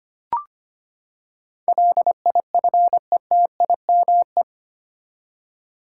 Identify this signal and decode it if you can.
Morse code, heard as a keyed tone, about 25 wpm, 700 Hz. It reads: LIFETIME
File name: Eight Letter Words - 25wpm